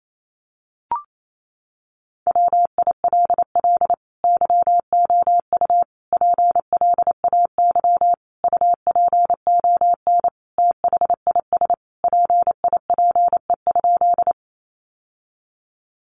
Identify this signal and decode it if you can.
Morse code, heard as a keyed tone, about 28 wpm, 700 Hz. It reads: WILL YOU PLAY UPON THIS PIPE?
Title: Sentences from Top 1500 Words 28wpm